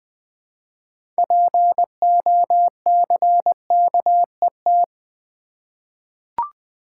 Morse code, heard as a keyed tone, about 20 wpm, 700 Hz. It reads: POCKET
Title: 6 Letter Words - First 3 Letters Given - 20wpm